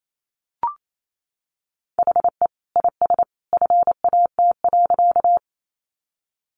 Morse code, heard as a keyed tone, about 28 wpm, 700 Hz. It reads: HE IS FAT.
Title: Sentences from Top 600 Words 28wpm